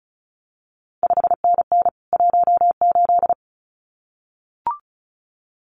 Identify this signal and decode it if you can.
Morse code, heard as a keyed tone, about 36 wpm, 700 Hz. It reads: 5NN 18
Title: CQ WW DX Contest Exchange Encore - 35wpm